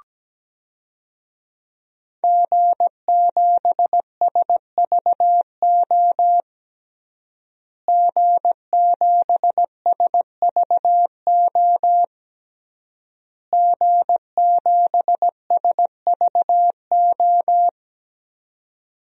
Morse code, heard as a keyed tone, about 17 wpm, 700 Hz. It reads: G7SVO G7SVO G7SVO
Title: DXCC-100 Call Signs - 5 Characters - Repeated 3x - 17wpm